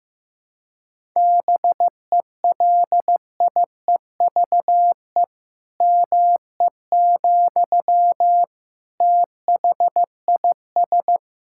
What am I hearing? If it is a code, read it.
Morse code, heard as a keyed tone, about 15 wpm, 700 Hz. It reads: BELIEVE ME, THIS
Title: Sentences from Top 1000 Words - Review Part 6 - 15wpm